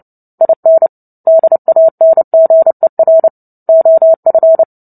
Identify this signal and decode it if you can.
Morse code, heard as a keyed tone, about 30 wpm, 650 Hz. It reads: IN DANGER OF
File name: Book Two / 03-Chapter 3 - 29 WPM